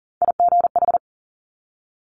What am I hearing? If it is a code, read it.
Morse code, heard as a keyed tone, about 39 wpm, 700 Hz. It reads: IGH
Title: Top Three Letter Combinations in Words 40wpm